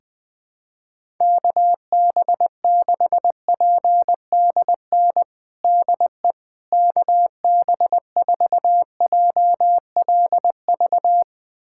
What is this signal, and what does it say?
Morse code, heard as a keyed tone, about 19 wpm, 700 Hz. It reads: KB6PDN DE KB4JLV
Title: USA Callsign DE Callsign - 6 Characters - 20wpm